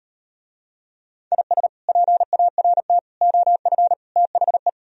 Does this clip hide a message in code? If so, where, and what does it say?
Yes: Morse code from 1.3 to 4.7 s, heard as a keyed tone, about 38 wpm, 700 Hz. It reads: IS PART OF THE